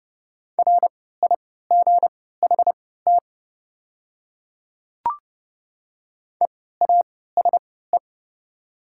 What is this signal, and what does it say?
Morse code, heard as a keyed tone, about 32 wpm, spaced as for 17 wpm, 690 Hz. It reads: RIGHT EASE
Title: 3x Character Spacing - Top 300 Words - 30wpm